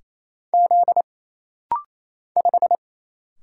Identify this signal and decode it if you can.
Morse code, heard as a keyed tone, about 28 wpm, 700 Hz. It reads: Z 5
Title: Single Letter-Number - Rapid-Fire 28wpm